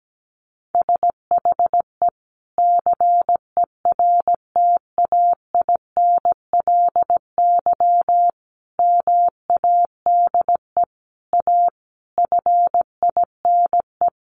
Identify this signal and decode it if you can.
Morse code, heard as a keyed tone, about 17 wpm, 700 Hz. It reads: SHE CERTAINLY MADE A FINE
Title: Sentences from Top 1100 Words 17wpm